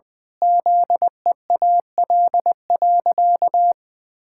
Morse code, heard as a keyed tone, about 20 wpm, 700 Hz. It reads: ZEAL.
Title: Pangram Challenge v2 - 20wpm